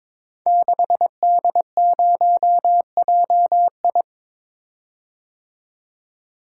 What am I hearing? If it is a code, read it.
Morse code, heard as a keyed tone, about 22 wpm, 700 Hz. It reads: 6D0JI
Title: Code Groups - 5 Letter:Numbers - 22wpm